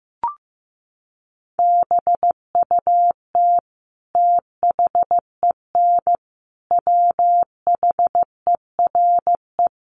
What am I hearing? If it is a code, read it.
Morse code, heard as a keyed tone, about 15 wpm, 700 Hz. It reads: BUT THEN WHERE
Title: Sentences from Top 1300 Words 15wpm